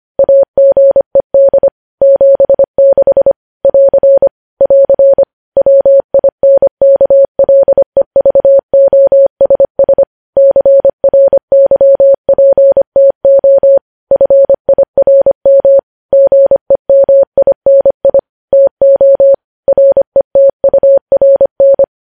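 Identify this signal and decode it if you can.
Morse code, heard as a keyed tone, about 25 wpm, 550 Hz. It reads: AGED 76 <AR> <AR> WINKLEVOSS CRYPTO FIRM GEMINI TO RETURN
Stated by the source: News headlines sent in Morse code at 25.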